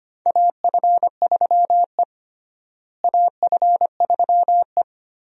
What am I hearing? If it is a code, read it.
Morse code, heard as a keyed tone, about 25 wpm, 700 Hz. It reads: AF3E AF3E
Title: USA Call Signs - 4 Characters - Repeated 3x - 25wpm